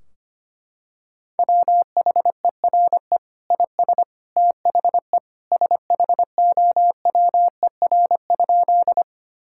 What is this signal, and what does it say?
Morse code, heard as a keyed tone, about 25 wpm, 700 Hz. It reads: WHERE IS THE SHOWER?
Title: Sentences from Top 1600 Words 25wpm